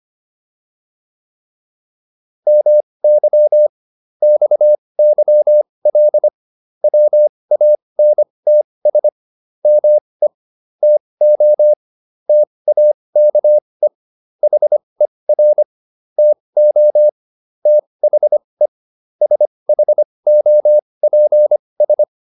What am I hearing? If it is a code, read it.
Morse code, heard as a keyed tone, about 25 wpm, spaced as for 20 wpm, 600 Hz. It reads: MY XYL WANTS ME TO TAKE HER TO THE SHOPS
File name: QSO406 20